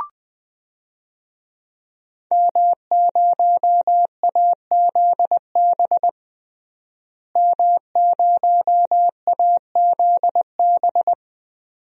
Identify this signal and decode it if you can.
Morse code, heard as a keyed tone, about 20 wpm, 700 Hz. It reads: M0AZB M0AZB
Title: DXCC-100 Call Signs - 5 Characters - Repeated 2x - 20wpm